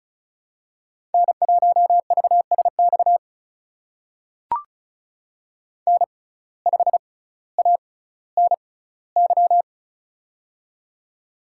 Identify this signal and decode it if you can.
Morse code, heard as a keyed tone, about 36 wpm, spaced as for 18 wpm, 700 Hz. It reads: N1VSX N5ANY
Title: USA Call Signs - 5 Characters - 6x Character Spacing - 35wpm